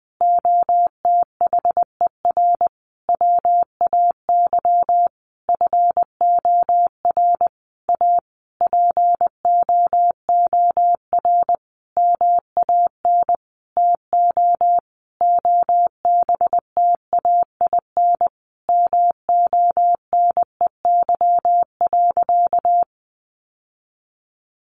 Morse code, heard as a keyed tone, about 20 wpm, 700 Hz. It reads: OTHER WAY FOR A POOR MAN TO OBTAIN MONEY.
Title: Sentences from Top 1300 Words 20wpm